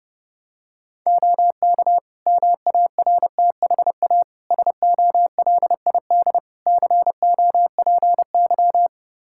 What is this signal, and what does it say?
Morse code, heard as a keyed tone, about 30 wpm, 700 Hz. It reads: OK MARTHA SOLID COPY
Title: Practice QSO - Exchange 3 - More Info - 30wpm